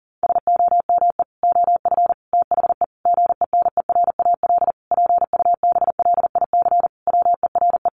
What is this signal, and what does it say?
Morse code, heard as a keyed tone, about 37 wpm, 700 Hz. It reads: SOME OF THE GENERAL PUBLIC WERE
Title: Sentences from Top 800 Words 40wpm